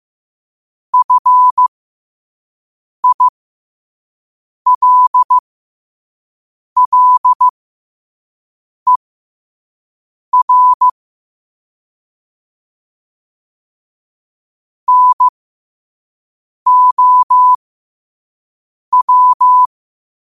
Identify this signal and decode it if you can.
Morse code, heard as a keyed tone, about 15 wpm, spaced as for 5 wpm, 1000 Hz. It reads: FILLER NOW
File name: qotdfiles / 5wpm